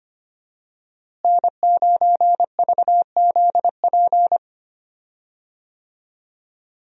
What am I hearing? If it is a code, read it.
Morse code, heard as a keyed tone, about 25 wpm, 700 Hz. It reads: N9VZP
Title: USA Call Signs - 5 Characters - Repeated 2x - 25wpm